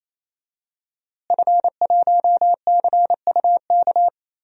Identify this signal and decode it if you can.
Morse code, heard as a keyed tone, about 28 wpm, 700 Hz. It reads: F1CUK
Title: DXCC-100 Call Signs - 5 Characters - Repeated 3x - 28wpm